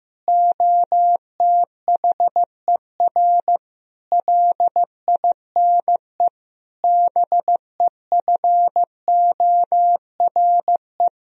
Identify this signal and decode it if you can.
Morse code, heard as a keyed tone, about 15 wpm, 700 Hz. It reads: OTHER LINE BEFORE
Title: Sentences from Top 1000 Words - Review Part 1 - 15wpm